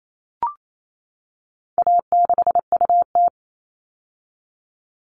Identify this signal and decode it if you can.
Morse code, heard as a keyed tone, about 28 wpm, 700 Hz. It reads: A6UT
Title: All Cut Numbers - 4-Digit 28wpm